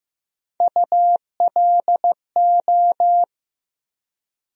Morse code, heard as a keyed tone, about 15 wpm, 700 Hz. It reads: ULO
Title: USA Call Sign Postfixes - 3 Letters 15wpm